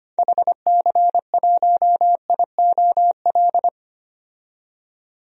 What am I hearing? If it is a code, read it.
Morse code, heard as a keyed tone, about 25 wpm, 700 Hz. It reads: HC1IOL
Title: DXCC-100 Call Signs - 6 Characters - 25wpm